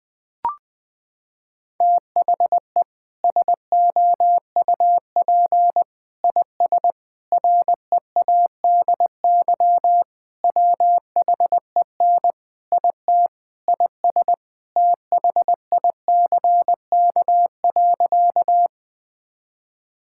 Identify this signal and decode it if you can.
Morse code, heard as a keyed tone, about 20 wpm, 700 Hz. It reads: THE SOUP IS READY WHEN IT IS THICK.
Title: Sentences from Top 1600 Words 20wpm